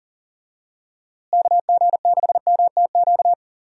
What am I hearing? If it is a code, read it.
Morse code, heard as a keyed tone, about 37 wpm, 700 Hz. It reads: KG6MTQ
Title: USA Call Signs - 6 Characters - Repeated 3x - 40wpm